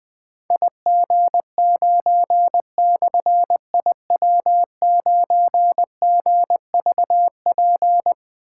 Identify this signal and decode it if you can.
Morse code, heard as a keyed tone, about 20 wpm, 700 Hz. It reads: IG9/IW9GVP
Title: Difficult Call Signs 20wpm